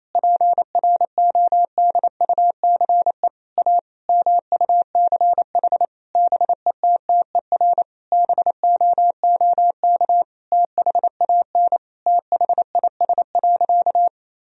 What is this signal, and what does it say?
Morse code, heard as a keyed tone, about 28 wpm, 700 Hz. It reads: PRODUCE A MUCH BETTER BOOK THAN THIS.
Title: Sentences from Top 1000 Words - Review Part 3 - 28wpm